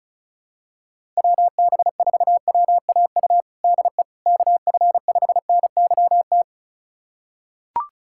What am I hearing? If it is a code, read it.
Morse code, heard as a keyed tone, about 36 wpm, 700 Hz. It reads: WB4WAU DE KF5NYT
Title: USA Callsign DE Callsign - 6 Characters - 35wpm